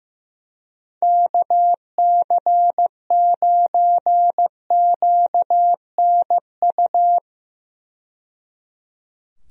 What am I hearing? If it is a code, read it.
Morse code, heard as a keyed tone, about 16 wpm, 700 Hz. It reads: KC9QNU
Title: USA Call Signs - 6 Characters - 15wpm